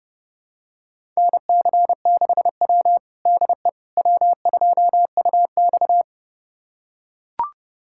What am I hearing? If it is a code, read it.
Morse code, heard as a keyed tone, about 30 wpm, 700 Hz. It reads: NC6W DE W2UX
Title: USA Callsign DE Callsign - 4 Characters - 30wpm